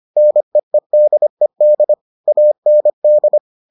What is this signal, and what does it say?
Morse code, heard as a keyed tone, about 25 wpm, 600 Hz. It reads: NEEDED AND